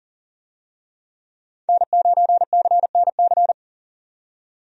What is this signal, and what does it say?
Morse code, heard as a keyed tone, about 40 wpm, 700 Hz. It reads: N9CNC
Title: USA Call Signs - 5 Characters - 40wpm